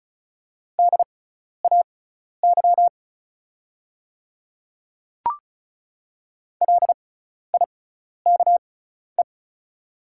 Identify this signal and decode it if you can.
Morse code, heard as a keyed tone, about 36 wpm, spaced as for 12 wpm, 700 Hz. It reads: DAY LIKE
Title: 6x Character Spacing - Top 100 Words - 35wpm